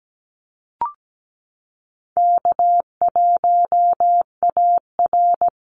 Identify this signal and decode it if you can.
Morse code, heard as a keyed tone, about 17 wpm, 700 Hz. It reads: K1AR
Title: Easy Call Signs Repeated 2x - 17wpm